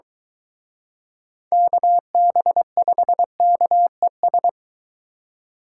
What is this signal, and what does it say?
Morse code, heard as a keyed tone, about 23 wpm, 700 Hz. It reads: KB5KES